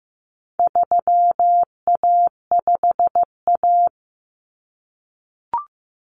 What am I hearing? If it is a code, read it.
Morse code, heard as a keyed tone, about 15 wpm, 700 Hz. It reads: 3A5A
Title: Common Cut Numbers - 4-Digit 15wpm